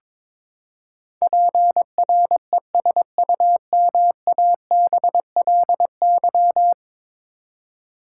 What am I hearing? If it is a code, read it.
Morse code, heard as a keyed tone, about 22 wpm, 700 Hz. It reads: PRESUMABLY
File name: practice / 22wpm / Ten Letter Words - 22wpm